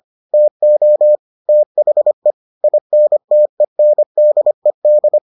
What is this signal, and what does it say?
Morse code, heard as a keyed tone, about 25 wpm, 600 Hz. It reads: TO THE INTENDED